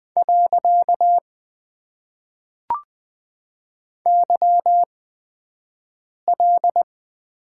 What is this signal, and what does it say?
Morse code, heard as a keyed tone, about 20 wpm, spaced as for 5 wpm, 700 Hz. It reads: . YL
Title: Top 90 QSO Elements - 8x Character Spacing - 20wpm